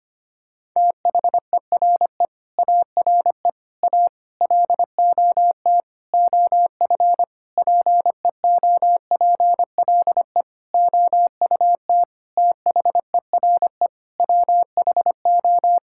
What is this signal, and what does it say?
Morse code, heard as a keyed tone, about 25 wpm, 700 Hz. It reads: THERE ARE A LOT OF PEOPLE OUT THERE WHO
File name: practice / 25wpm / Sentences with 9-Word Phrases - 25wpm